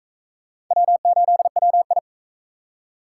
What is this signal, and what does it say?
Morse code, heard as a keyed tone, about 40 wpm, 700 Hz. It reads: W8WI